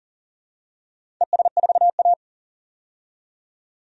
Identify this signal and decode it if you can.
Morse code, heard as a keyed tone, about 40 wpm, 700 Hz. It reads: ES4A
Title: DXCC-100 Call Signs - 4 Characters - 40wpm